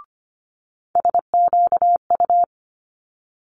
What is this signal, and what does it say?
Morse code, heard as a keyed tone, about 25 wpm, 700 Hz. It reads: SQU
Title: Multi-letter Phonemes Quick Fire 25wpm